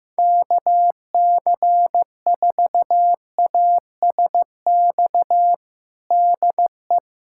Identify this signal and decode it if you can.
Morse code, heard as a keyed tone, about 15 wpm, 700 Hz. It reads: KC4ASX DE
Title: USA Callsign DE Callsign 15wpm